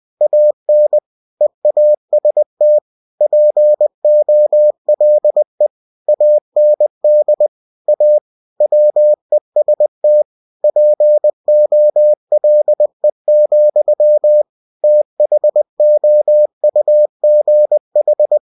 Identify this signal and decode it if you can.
Morse code, heard as a keyed tone, about 20 wpm, 600 Hz. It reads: AN EAST POLE AND A WEST POLE, THOUGH
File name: cw-books / winnie-the-pooh / winnie-20wpm-page-0093